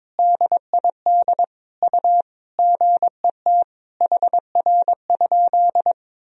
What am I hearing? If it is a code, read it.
Morse code, heard as a keyed tone, about 21 wpm, 700 Hz. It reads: DID U GET HR?
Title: Sentences from Top 200 Words and Abbreviations 22wpm